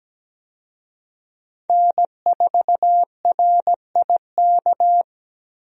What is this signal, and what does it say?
Morse code, heard as a keyed tone, about 17 wpm, 700 Hz. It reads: N4RIK
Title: USA Call Signs - 5 Characters - 17wpm